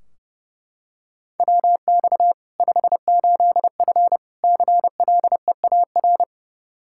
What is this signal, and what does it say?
Morse code, heard as a keyed tone, about 32 wpm, 700 Hz. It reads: WX 58F CLEAR
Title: Practice QSO - Exchange 2 - Info - 30wpm